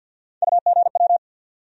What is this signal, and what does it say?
Morse code, heard as a keyed tone, about 51 wpm, 700 Hz. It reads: UGW
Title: USA Call Sign Postfixes - 3 Letters 50wpm